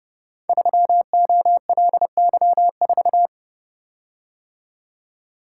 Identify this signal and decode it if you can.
Morse code, heard as a keyed tone, about 30 wpm, 700 Hz. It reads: 3OLY4
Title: Code Groups - 5 Letter:Numbers - 30wpm